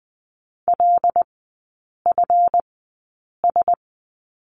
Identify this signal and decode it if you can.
Morse code, heard as a keyed tone, about 20 wpm, spaced as for 8 wpm, 690 Hz. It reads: LFS
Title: 145 Lesson 24e - Letter y - Three Characters 20wpm